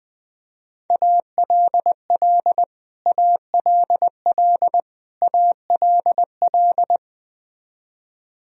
Morse code, heard as a keyed tone, about 21 wpm, 700 Hz. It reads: ALL ALL ALL
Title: Top 100 Words Repeated 20wpm